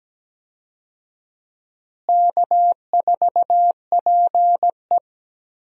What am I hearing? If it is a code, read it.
Morse code, heard as a keyed tone, about 17 wpm, 700 Hz. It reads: K4PE
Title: USA Call Signs - 4 Characters - Repeated 3x - 17wpm